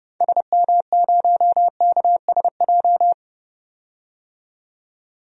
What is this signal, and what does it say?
Morse code, heard as a keyed tone, about 31 wpm, 700 Hz. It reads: SM0KSJ
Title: DXCC-100 Call Signs - 6 Characters - 30wpm